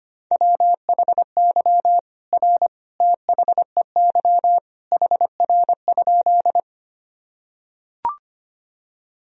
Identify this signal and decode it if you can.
Morse code, heard as a keyed tone, about 25 wpm, 700 Hz. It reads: WHY R THEY HR?
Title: Sentences from Top 200 Words and Abbreviations 25wpm